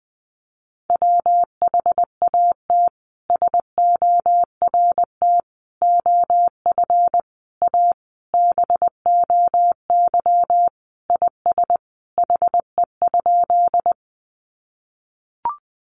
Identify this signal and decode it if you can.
Morse code, heard as a keyed tone, about 20 wpm, 700 Hz. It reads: WHAT SORT OF A BOY IS HE?
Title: Sentences from Top 1300 Words 20wpm